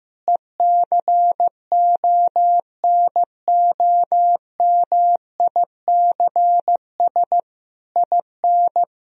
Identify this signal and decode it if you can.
Morse code, heard as a keyed tone, about 16 wpm, 700 Hz. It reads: ECONOMICS IN
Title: Sentences from Top 1000 Words - Review Part 3 - 15wpm